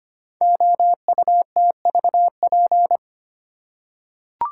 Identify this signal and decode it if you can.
Morse code, heard as a keyed tone, about 25 wpm, 700 Hz. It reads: OUTVP
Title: Code Groups - 5 Letter:Numbers - 25wpm